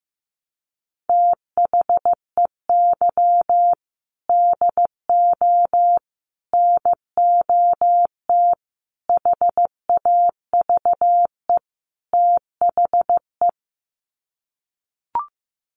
Morse code, heard as a keyed tone, about 15 wpm, 700 Hz. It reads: THEY DO NOT HAVE THE
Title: Sets of 5 Words 15wpm